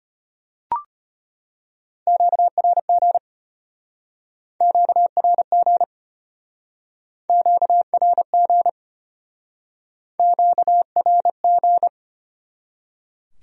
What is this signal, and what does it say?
Morse code, heard as a keyed tone, about 31 wpm, 700 Hz. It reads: QRG QRG QRG QRG